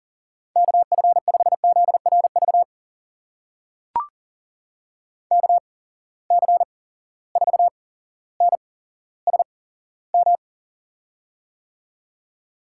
Morse code, heard as a keyed tone, about 42 wpm, spaced as for 17 wpm, 700 Hz. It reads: KF5ZRV KC4NSM
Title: USA Call Signs - 6 Characters - 8x Character Spacing - 40wpm